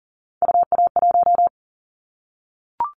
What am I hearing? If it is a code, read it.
Morse code, heard as a keyed tone, about 40 wpm, 700 Hz. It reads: UA1
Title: Callsign Prefixes for Top 100 DXCC - 40wpm